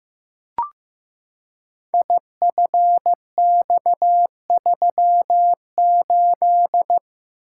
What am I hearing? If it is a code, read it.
Morse code, heard as a keyed tone, about 15 wpm, 700 Hz. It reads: IFX38
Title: Code Groups - 5 Letter:Numbers - 15wpm